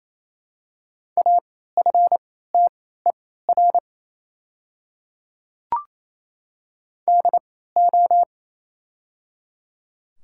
Morse code, heard as a keyed tone, about 28 wpm, spaced as for 16 wpm, 690 Hz. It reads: AFTER DO
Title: 3x Character Spacing - Top 500 Words - 28wpm